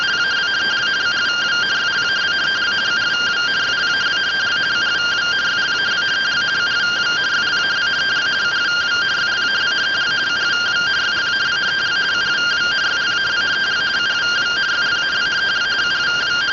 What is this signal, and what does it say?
Two-tone teleprinter signal, about 170 Hz shift.